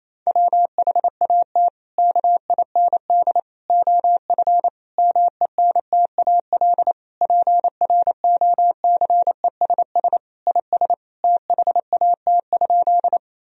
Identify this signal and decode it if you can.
Morse code, heard as a keyed tone, about 27 wpm, 700 Hz. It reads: WHAT KIND OF MENTAL PROCESS IS THAT?
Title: Sentences from Top 1000 Words - Review Part 4 - 28wpm